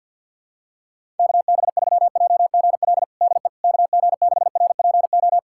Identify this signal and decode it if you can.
Morse code, heard as a keyed tone, about 47 wpm, 700 Hz. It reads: KB3JGL DE KG6RPO
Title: USA Callsign DE Callsign 50wpm